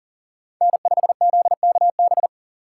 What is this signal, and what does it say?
Morse code, heard as a keyed tone, about 40 wpm, 700 Hz. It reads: N5ZKB
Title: USA Call Signs - 5 Characters - 4x Character Spacing - 40wpm